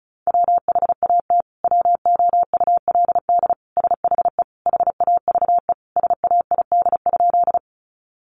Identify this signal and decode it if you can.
Morse code, heard as a keyed tone, about 34 wpm, 700 Hz. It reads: WHAT WOULD SHE HAVE SAID?
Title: Sentences from Top 100 Words 35wpm